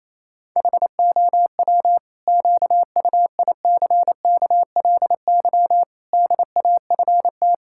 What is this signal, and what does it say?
Morse code, heard as a keyed tone, about 28 wpm, 700 Hz. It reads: HOW QUICKLY DAFT
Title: Pangram Challenge v2 - 28wpm